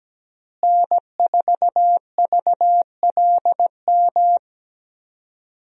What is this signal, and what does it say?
Morse code, heard as a keyed tone, about 17 wpm, 700 Hz. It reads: N4VLM
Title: USA Call Signs - 5 Characters - Repeated 2x - 17wpm